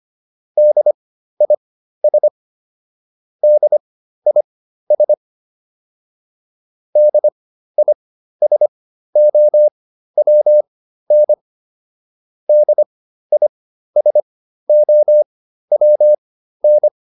Morse code, heard as a keyed tone, about 25 wpm, spaced as for 13 wpm, 600 Hz. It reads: DIS DIS DISOWN DISOWN
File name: DIS6 13